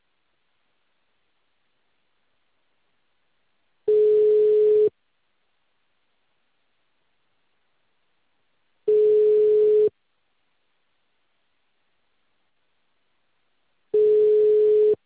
Ringback tone (Europe, 425 Hz).
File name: out- -2044- - - .28272